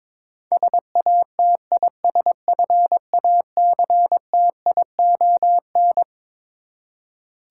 Morse code, heard as a keyed tone, about 22 wpm, 700 Hz. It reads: SATISFACTION
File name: Twelve Letter Words - 22wpm